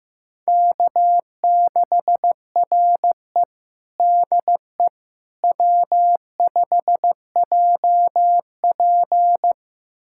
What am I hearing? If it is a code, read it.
Morse code, heard as a keyed tone, about 15 wpm, 700 Hz. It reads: K6RE DE W5JP
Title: USA Callsign DE Callsign - 4 Characters - 15wpm